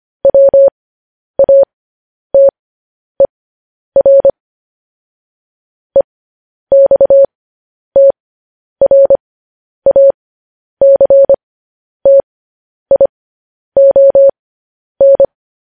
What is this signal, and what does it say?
Morse code, heard as a keyed tone, about 25 wpm, spaced as for 10 wpm, 550 Hz. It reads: WATER EXTRACTION
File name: podcast / News headlines-1x-25x10